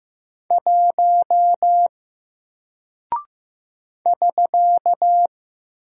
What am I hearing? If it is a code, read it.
Morse code, heard as a keyed tone, about 15 wpm, 700 Hz. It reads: 1 <SK>
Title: Single Character (Letter:Number:Punctuation:Prosign) Alt 15wpm